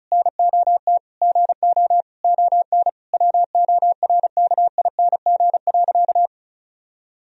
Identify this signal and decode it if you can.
Morse code, heard as a keyed tone, about 34 wpm, 700 Hz. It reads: NOT GO ON WORKING.
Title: Sentences from Top 1000 Words - Review Part 4 - 35wpm